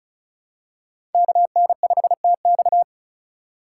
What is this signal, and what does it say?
Morse code, heard as a keyed tone, about 34 wpm, 700 Hz. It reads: KN5TX
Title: K1USN SST QSO Practice - 35wpm